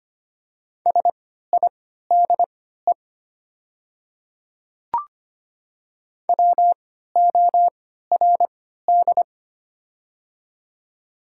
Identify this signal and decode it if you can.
Morse code, heard as a keyed tone, about 25 wpm, spaced as for 14 wpm, 690 Hz. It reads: SIDE WORD
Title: 3x Character Spacing - Top 300 Words - 25wpm